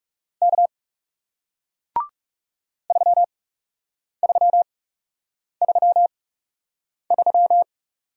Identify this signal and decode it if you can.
Morse code, heard as a keyed tone, about 40 wpm, spaced as for 8 wpm, 700 Hz. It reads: K 3333